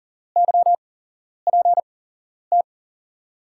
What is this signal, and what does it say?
Morse code, heard as a keyed tone, about 40 wpm, spaced as for 11 wpm, 700 Hz. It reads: YPT
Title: USA Call Signs - 5 Characters - 8x Character Spacing - 40wpm